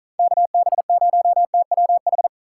Morse code, heard as a keyed tone, about 44 wpm, 700 Hz. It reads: KB0TWH